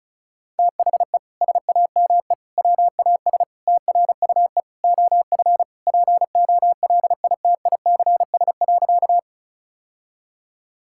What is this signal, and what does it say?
Morse code, heard as a keyed tone, about 36 wpm, 700 Hz. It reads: THE SAME WAS TRUE OF POLITICS.
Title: Sentences from Top 1000 Words - Review Part 6 - 35wpm